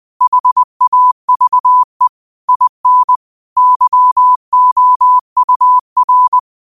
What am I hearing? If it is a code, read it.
Morse code, heard as a keyed tone, about 20 wpm, 1000 Hz. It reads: HAVE IN YOUR